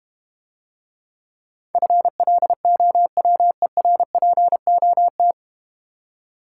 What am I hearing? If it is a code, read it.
Morse code, heard as a keyed tone, about 32 wpm, 700 Hz. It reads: FLOWERPOT